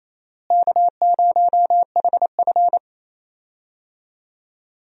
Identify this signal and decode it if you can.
Morse code, heard as a keyed tone, about 28 wpm, 700 Hz. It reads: K0HF
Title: USA Call Signs - 4 Characters - Repeated 2x - 28wpm